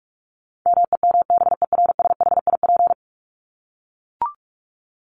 Morse code, heard as a keyed tone, about 44 wpm, 700 Hz. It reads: MEMBERSHIP
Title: Top 1251-1500 Words 45wpm